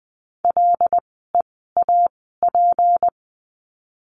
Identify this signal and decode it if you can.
Morse code, heard as a keyed tone, about 20 wpm, spaced as for 15 wpm, 700 Hz. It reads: LEAP
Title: 153 Lesson 25f - Letter p - Words 20wpm